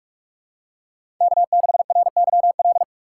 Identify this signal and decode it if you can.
Morse code, heard as a keyed tone, about 46 wpm, 700 Hz. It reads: K6RYL
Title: USA Call Signs - 5 Characters - 4x Character Spacing - 45wpm